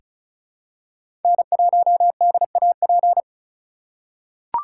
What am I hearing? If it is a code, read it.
Morse code, heard as a keyed tone, about 36 wpm, 700 Hz. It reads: N1DAP
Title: USA Call Signs - 5 Characters - 3x Character Spacing - 35wpm